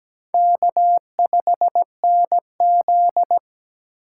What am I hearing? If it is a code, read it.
Morse code, heard as a keyed tone, about 20 wpm, 700 Hz. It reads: K5NZ
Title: Speed Racing - USA 1x2 Call Signs 17wpm